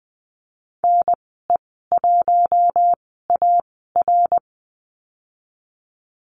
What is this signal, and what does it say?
Morse code, heard as a keyed tone, about 20 wpm, spaced as for 14 wpm, 700 Hz. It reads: NE1AR
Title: 062 Lesson 11g - Letter h - Call Signs 20wpm